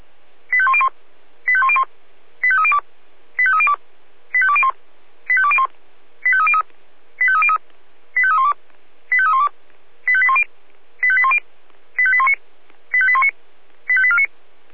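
Five-tone selective call (ZVEI), heard as quick runs of five tones, tones of about 70 ms. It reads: 85101 85101 85302 85302 85201 85201 85303 85303 85312 85312 85810 85810 85819 85819 85849
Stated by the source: ZVEI five-tone selective call received on an SDR.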